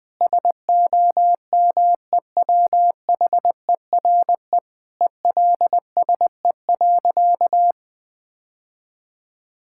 Morse code, heard as a keyed tone, about 20 wpm, 700 Hz. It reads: SOMEWHERE ELSE.
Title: Sentences with 5-Word Phrases - 20wpm